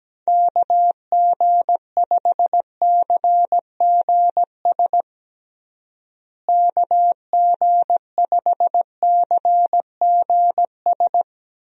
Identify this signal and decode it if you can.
Morse code, heard as a keyed tone, about 17 wpm, 700 Hz. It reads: KG5CGS KG5CGS
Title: USA Call Signs - 6 Characters - Repeated 3x - 17wpm